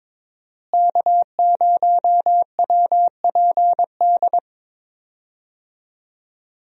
Morse code, heard as a keyed tone, about 22 wpm, 700 Hz. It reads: K0WPD
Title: USA Call Signs - 5 Characters - Repeated 2x - 22wpm